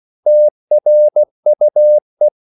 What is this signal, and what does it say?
Morse code, heard as a keyed tone, about 16 wpm, 600 Hz. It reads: TRUE
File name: CW / Practice / EnglishQuotations0216